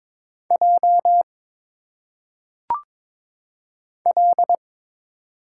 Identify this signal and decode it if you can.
Morse code, heard as a keyed tone, about 23 wpm, 700 Hz. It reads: J L
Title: Single Letters - Phonetic Pronunciation 22wpm